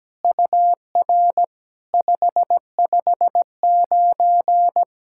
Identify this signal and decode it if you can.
Morse code, heard as a keyed tone, about 17 wpm, 700 Hz. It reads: UR 559
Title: SOTA QSO Practice 17wpm